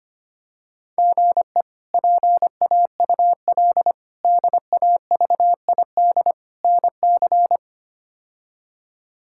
Morse code, heard as a keyed tone, about 25 wpm, 700 Hz. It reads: GE PAUL DAVID NC
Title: K1USN SST QSO Practice - 25wpm